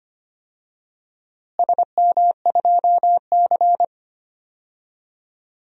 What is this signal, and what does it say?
Morse code, heard as a keyed tone, about 25 wpm, 700 Hz. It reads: SM2C
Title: DXCC-100 Call Signs - 4 Characters - Repeated 3x - 25wpm